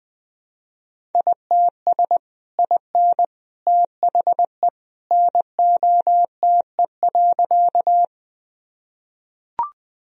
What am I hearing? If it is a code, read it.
Morse code, heard as a keyed tone, about 20 wpm, 700 Hz. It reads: ITS IN THE NOTE.
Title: Sentences from Top 1000 Words - Review Part 2 - 20wpm